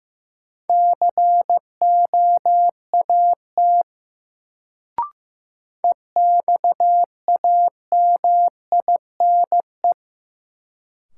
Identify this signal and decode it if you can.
Morse code, heard as a keyed tone, about 15 wpm, 700 Hz. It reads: COAT EXAMINE
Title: Sentences from Top 1600 Words - New Word Drill Alt - 15wpm